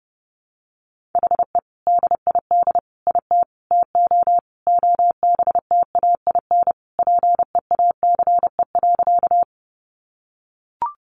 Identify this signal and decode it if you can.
Morse code, heard as a keyed tone, about 30 wpm, 700 Hz. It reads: HE DID IT TO OBTAIN PEACE.
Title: Sentences from Top 1500 Words 30wpm